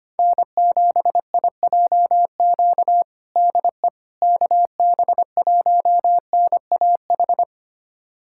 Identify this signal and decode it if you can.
Morse code, heard as a keyed tone, about 25 wpm, 700 Hz. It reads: N7IJQ DE KB1NAH
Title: USA Callsign DE Callsign 25wpm